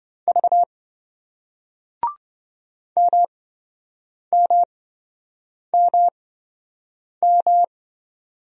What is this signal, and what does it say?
Morse code, heard as a keyed tone, about 28 wpm, spaced as for 7 wpm, 700 Hz. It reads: V MMMM